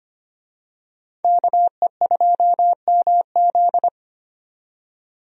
Morse code, heard as a keyed tone, about 25 wpm, 700 Hz. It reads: KE2MZ
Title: USA Call Signs - 5 Characters - 25wpm